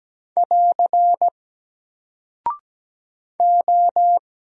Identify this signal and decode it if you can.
Morse code, heard as a keyed tone, about 17 wpm, 700 Hz. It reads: <AR> O
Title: Single Character (Letter:Number:Punctuation:Prosign) Alt 17wpm